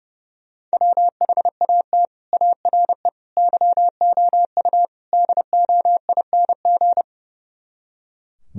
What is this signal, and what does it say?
Morse code, heard as a keyed tone, about 35 wpm, 700 Hz. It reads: WHAT ARE YOU DOING
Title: Speed-Racing - Sets of 4 Words 30wpm